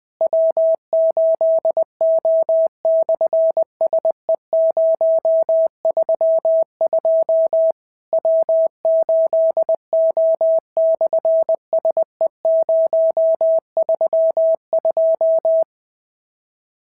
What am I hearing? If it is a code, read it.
Morse code, heard as a keyed tone, about 20 wpm, 650 Hz. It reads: W8O/SE032 W8O/SE032
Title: SOTA QSO Practice 20wpm